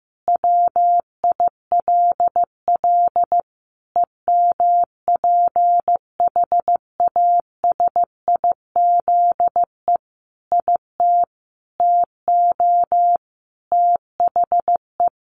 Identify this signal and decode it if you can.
Morse code, heard as a keyed tone, about 15 wpm, 700 Hz. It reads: WILL EMPHASIZE IT TO THE
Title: Sentences from Top 1500 Words 15wpm